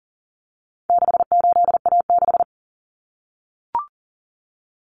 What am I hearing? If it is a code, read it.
Morse code, heard as a keyed tone, about 40 wpm, 700 Hz. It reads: 68A6
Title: Common Cut Numbers - 4-Digit 40wpm